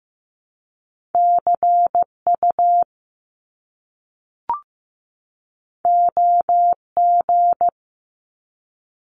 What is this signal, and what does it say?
Morse code, heard as a keyed tone, about 15 wpm, 700 Hz. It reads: CU OG
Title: Top Two Letter Combinations in Words 15wpm